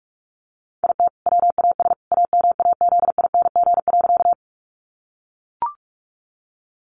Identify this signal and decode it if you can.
Morse code, heard as a keyed tone, about 44 wpm, 700 Hz. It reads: IT WAS AMAZING.
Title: Sentences from Top 1500 Words 45wpm